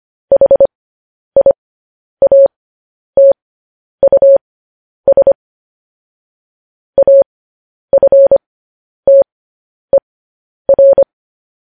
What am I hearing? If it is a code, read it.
Morse code, heard as a keyed tone, about 25 wpm, spaced as for 10 wpm, 550 Hz. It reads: HIATUS AFTER
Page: news-headlines-in-morse-code-at-10-wpm